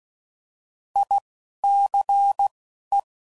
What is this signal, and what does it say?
Morse code, heard as a keyed tone, about 16 wpm, spaced as for 11 wpm, 800 Hz. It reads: ICE